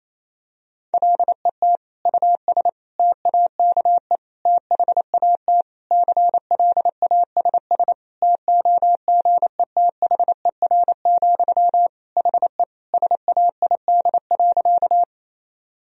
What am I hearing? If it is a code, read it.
Morse code, heard as a keyed tone, about 28 wpm, 700 Hz. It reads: LET US TAKE THAT CLASS TOGETHER, HE SAID.
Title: Sentences from Top 1000 Words - Review Part 2 - 28wpm